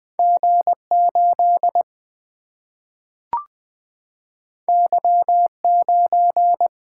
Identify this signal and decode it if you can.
Morse code, heard as a keyed tone, about 20 wpm, 700 Hz. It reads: G 8 Y 9
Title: Single-Letter Single-Number 20wpm